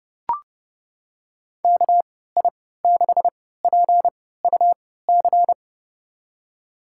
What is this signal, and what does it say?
Morse code, heard as a keyed tone, about 30 wpm, spaced as for 17 wpm, 700 Hz. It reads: KI6PUC
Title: USA Call Signs - 6 Characters - 3x Character Spacing - 30wpm